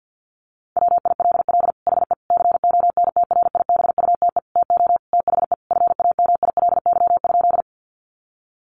Morse code, heard as a keyed tone, about 50 wpm, 700 Hz. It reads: WILL HE CONTRIBUTE TO THE FAMILY?
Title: Sentences from Top 1400 Words 50wpm